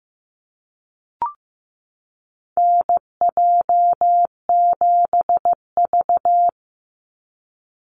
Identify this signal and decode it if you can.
Morse code, heard as a keyed tone, about 15 wpm, 700 Hz. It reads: NJ7V
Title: NA SOTA Chaser Callsigns 15wpm